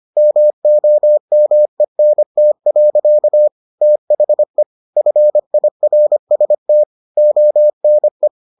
Morse code, heard as a keyed tone, about 25 wpm, 600 Hz. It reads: MOMENT. THE FIRST ONE